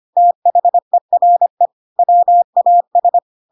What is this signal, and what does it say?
Morse code, heard as a keyed tone, about 25 wpm, 700 Hz. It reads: THERE WAS
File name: Pinocchio / Pinocchio ch3 25 25 ik1hge